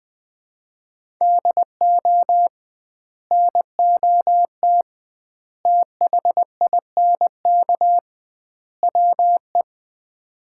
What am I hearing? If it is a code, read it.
Morse code, heard as a keyed tone, about 21 wpm, 700 Hz. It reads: DO NOT THINK WE
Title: 2x Word-Spacing - Sets of 4 Words - 20wpm